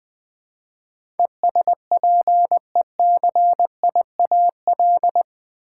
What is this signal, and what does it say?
Morse code, heard as a keyed tone, about 20 wpm, 700 Hz. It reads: ESPECIAL
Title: Build A Word - 4 Letters At A Time - 10-Letter Words - 20wpm